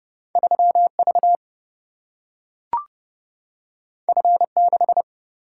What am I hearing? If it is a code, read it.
Morse code, heard as a keyed tone, about 29 wpm, 700 Hz. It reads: 3V F6
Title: Single-Letter Single-Number 30wpm